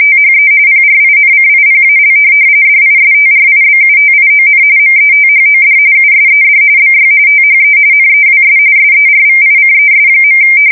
RTTY, about 45.45 baud. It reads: THE PRINCIPAL DISTRIBUTION METHOD USED BY THE NEWS SERVICES.
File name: fsk170 / teletype-RTTY